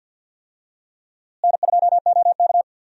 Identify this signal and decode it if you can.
Morse code, heard as a keyed tone, about 52 wpm, 700 Hz. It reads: N2OK
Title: USA Call Signs - 4 Characters - Repeated 3x - 50wpm